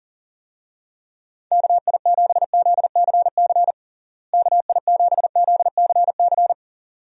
Morse code, heard as a keyed tone, about 40 wpm, 700 Hz. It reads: KI7ZCC KI7ZCC
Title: USA Call Signs - 6 Characters - Repeated 2x - 40wpm